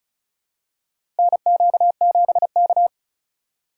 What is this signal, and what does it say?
Morse code, heard as a keyed tone, about 36 wpm, 700 Hz. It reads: NQ7K
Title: USA Call Signs - 4 Characters - 2x Character Spacing - 35wpm